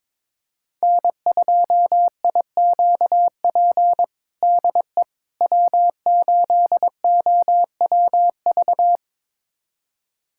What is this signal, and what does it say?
Morse code, heard as a keyed tone, about 21 wpm, 700 Hz. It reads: N2IQP DE W8OWV
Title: USA Callsign DE Callsign - 5 Characters - 22wpm